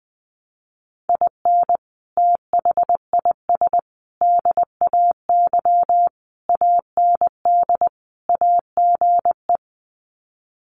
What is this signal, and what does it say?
Morse code, heard as a keyed tone, about 20 wpm, 700 Hz. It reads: IN THIS DAY AND AGE
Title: Sets of 5 Words Encore 20wpm